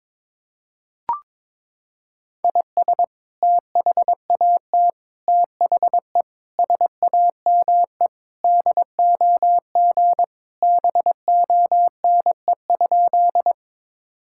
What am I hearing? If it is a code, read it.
Morse code, heard as a keyed tone, about 22 wpm, 700 Hz. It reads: IS THAT THE SAME DOG BONE?
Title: Sentences from Top 1600 Words 22wpm